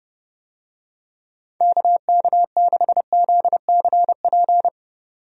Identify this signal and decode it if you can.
Morse code, heard as a keyed tone, about 31 wpm, 700 Hz. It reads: KK6ZCP
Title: USA Call Signs - 6 Characters - 30wpm